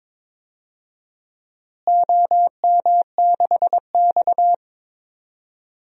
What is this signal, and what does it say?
Morse code, heard as a keyed tone, about 22 wpm, 700 Hz. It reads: OM6X
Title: DXCC-100 Call Signs - 4 Characters - Repeated 3x - 22wpm